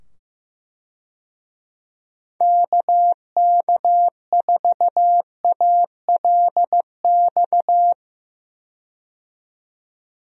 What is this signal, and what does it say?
Morse code, heard as a keyed tone, about 15 wpm, 700 Hz. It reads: KK4ALX
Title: USA Call Signs - 6 Characters - 15wpm